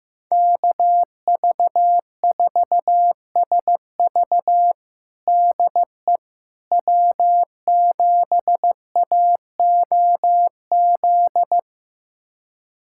Morse code, heard as a keyed tone, about 16 wpm, 700 Hz. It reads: KV4SV DE W7AOZ
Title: USA Callsign DE Callsign - 5 Characters - 15wpm